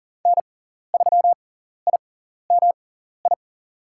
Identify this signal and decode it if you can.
Morse code, heard as a keyed tone, about 40 wpm, spaced as for 14 wpm, 700 Hz. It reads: N3IMI
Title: USA Call Signs - 5 Characters - 6x Character Spacing - 40wpm